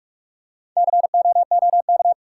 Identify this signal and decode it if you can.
Morse code, heard as a keyed tone, about 47 wpm, 700 Hz. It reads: COOK
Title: Four Letter Words - 45wpm